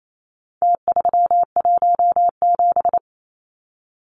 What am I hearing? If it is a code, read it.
Morse code, heard as a keyed tone, about 28 wpm, 700 Hz. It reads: T317
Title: Numbers - Contest Style - 4-Digit 28wpm